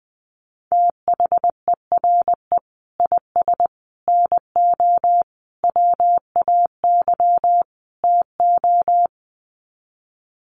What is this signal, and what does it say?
Morse code, heard as a keyed tone, about 20 wpm, 700 Hz. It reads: THERE IS NO WAY TO
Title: Sets of 5 Words Encore 20wpm